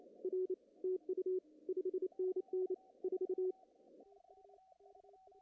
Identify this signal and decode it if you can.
Morse code, heard as a keyed tone, about 27 wpm, 340 Hz. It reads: R TU 5NN 4
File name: CW / cqww-cw-k5tr